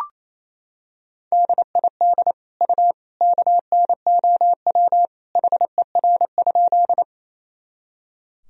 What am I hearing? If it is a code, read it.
Morse code, heard as a keyed tone, about 28 wpm, 700 Hz. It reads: DID U KNOW HER?
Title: Sentences from Top 100 Words and Abbreviations 28wpm